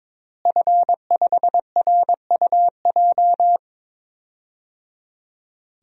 Morse code, heard as a keyed tone, about 22 wpm, 700 Hz. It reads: F5RUJ
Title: DXCC-100 Call Signs - 5 Characters - Repeated 3x - 22wpm